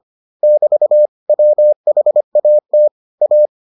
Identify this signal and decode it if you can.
Morse code, heard as a keyed tone, about 25 wpm, 600 Hz. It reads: <BT> WHAT A